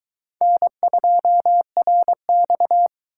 Morse code, heard as a keyed tone, about 22 wpm, 700 Hz. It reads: N2RX